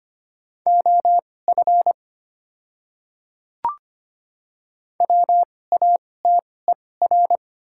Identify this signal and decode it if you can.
Morse code, heard as a keyed tone, about 25 wpm, spaced as for 18 wpm, 690 Hz. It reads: OF WATER
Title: 2x Character Spacing - Top 100 Words - 25wpm